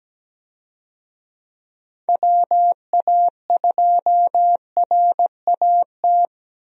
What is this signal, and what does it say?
Morse code, heard as a keyed tone, about 17 wpm, 700 Hz. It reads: WA2RAT
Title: USA Call Signs - 6 Characters - 17wpm